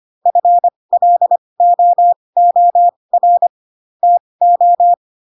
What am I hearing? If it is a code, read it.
Morse code, heard as a keyed tone, about 25 wpm, spaced as for 20 wpm, 700 Hz. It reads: FLOOR TO